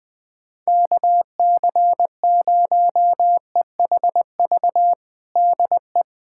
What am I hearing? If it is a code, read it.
Morse code, heard as a keyed tone, about 20 wpm, 700 Hz. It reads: KC0EHV DE
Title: USA Callsign DE Callsign - 6 Characters - 20wpm